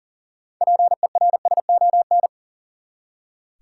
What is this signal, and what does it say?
Morse code, heard as a keyed tone, about 40 wpm, 700 Hz. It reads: PERSON
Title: Six Letter Words - 40wpm